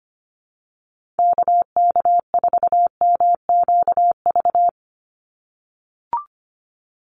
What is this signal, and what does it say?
Morse code, heard as a keyed tone, about 25 wpm, 700 Hz. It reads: KK4MQV
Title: USA Call Signs 25wpm